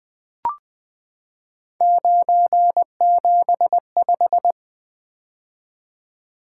Morse code, heard as a keyed tone, about 20 wpm, 700 Hz. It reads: 975
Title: Three Letter-Number Combinations - 20wpm